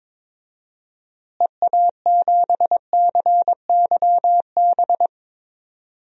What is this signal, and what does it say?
Morse code, heard as a keyed tone, about 22 wpm, 700 Hz. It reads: EA7CYB
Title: DXCC-100 Call Signs - 6 Characters - Repeated 3x - 22wpm